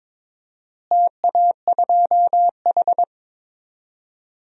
Morse code, heard as a keyed tone, about 22 wpm, 700 Hz. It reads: TA2H
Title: DXCC-100 Call Signs - 4 Characters - Repeated 2x - 22wpm